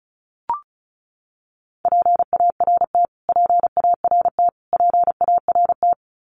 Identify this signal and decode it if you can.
Morse code, heard as a keyed tone, about 34 wpm, 700 Hz. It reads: PART PART PART
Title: Top 500 Words Repeated 35wpm